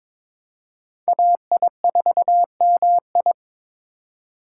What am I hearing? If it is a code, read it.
Morse code, heard as a keyed tone, about 22 wpm, 700 Hz. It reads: AI4MI
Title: K1USN SST QSO Practice - 22wpm